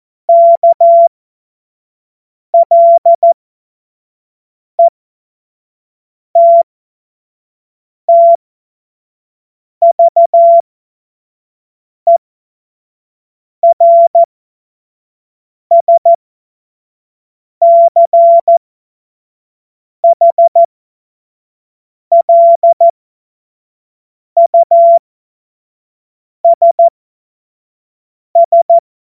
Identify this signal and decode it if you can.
Morse code, heard as a keyed tone, about 14 wpm, spaced as for 5 wpm, 680 Hz. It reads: KLETTVERSCHLUSS
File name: Morsekurs / Witze / Witz 22. August 2023 25 F70